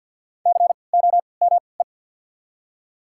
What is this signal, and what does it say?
Morse code, heard as a keyed tone, about 53 wpm, spaced as for 29 wpm, 700 Hz. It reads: COME
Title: 3x Character Spacing - Top 500 Words - 50wpm